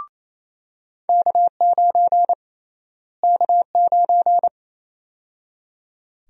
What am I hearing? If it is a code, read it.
Morse code, heard as a keyed tone, about 28 wpm, 700 Hz. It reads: K9 K9
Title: USA Call Sign Prefixes - Repeated 2x - 28wpm